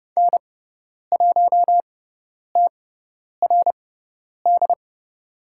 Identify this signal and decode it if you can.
Morse code, heard as a keyed tone, about 30 wpm, spaced as for 10 wpm, 700 Hz. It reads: N1TRD
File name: Lesson 18g - Letter m - Call Signs 30wpm